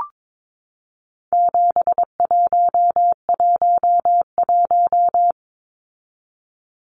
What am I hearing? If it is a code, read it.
Morse code, heard as a keyed tone, about 22 wpm, 700 Hz. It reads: 7111
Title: Numbers - 4-Digit 22wpm